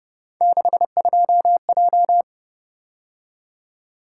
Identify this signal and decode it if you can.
Morse code, heard as a keyed tone, about 29 wpm, 700 Hz. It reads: 62J
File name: Speed-Racing - Three Letter-Number Combinations - 25wpm